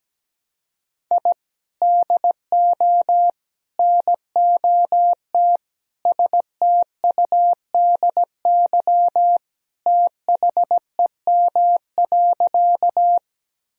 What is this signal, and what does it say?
Morse code, heard as a keyed tone, about 17 wpm, 700 Hz. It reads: I DO NOT STUDY THEM.
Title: Sentences from Top 1000 Words - Review Part 1 - 17wpm